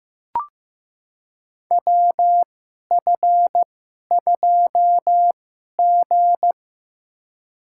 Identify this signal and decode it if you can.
Morse code, heard as a keyed tone, about 15 wpm, spaced as for 11 wpm, 700 Hz. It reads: WF2G
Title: USA Call Signs - 4 Characters - 2x Character Spacing - 15wpm